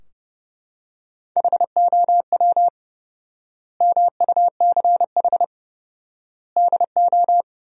Morse code, heard as a keyed tone, about 32 wpm, 700 Hz. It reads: HOW MUCH DO
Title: 4x Word-Spacing - Sentences from Top 200 Words - 30wpm